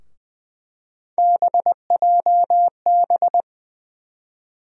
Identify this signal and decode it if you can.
Morse code, heard as a keyed tone, about 20 wpm, 700 Hz. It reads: BJB
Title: USA Call Sign Postfixes - 3 Letters 20wpm